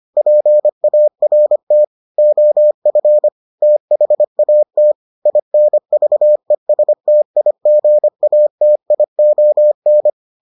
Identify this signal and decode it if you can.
Morse code, heard as a keyed tone, about 25 wpm, 600 Hz. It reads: PART OF THAT INVESTIGATION